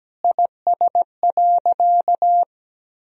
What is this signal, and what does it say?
Morse code, heard as a keyed tone, about 17 wpm, 700 Hz. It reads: IS.
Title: Sentences from Top 200 Words and Abbreviations 17wpm